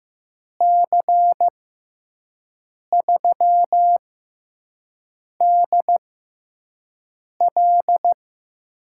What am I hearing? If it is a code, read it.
Morse code, heard as a keyed tone, about 15 wpm, spaced as for 5 wpm, 700 Hz. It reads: C3DL
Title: USA Call Signs - 6 Characters - 6x Character Spacing - 15wpm